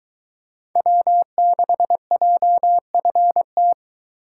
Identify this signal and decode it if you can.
Morse code, heard as a keyed tone, about 22 wpm, 700 Hz. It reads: W6JFT